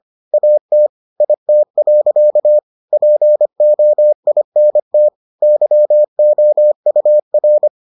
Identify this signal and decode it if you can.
Morse code, heard as a keyed tone, about 25 wpm, 600 Hz. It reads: AT IT. POINT YOUR